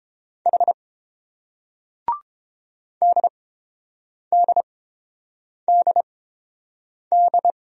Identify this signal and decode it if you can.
Morse code, heard as a keyed tone, about 31 wpm, spaced as for 7 wpm, 700 Hz. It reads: H DDDD